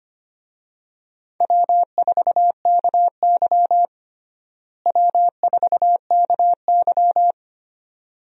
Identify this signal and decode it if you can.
Morse code, heard as a keyed tone, about 25 wpm, 700 Hz. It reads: W4KY W4KY
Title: USA Call Signs - 4 Characters - Repeated 2x - 25wpm